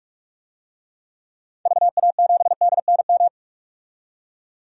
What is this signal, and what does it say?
Morse code, heard as a keyed tone, about 42 wpm, 700 Hz. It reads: VA7DNM
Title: DXCC-100 Call Signs - 6 Characters - 45wpm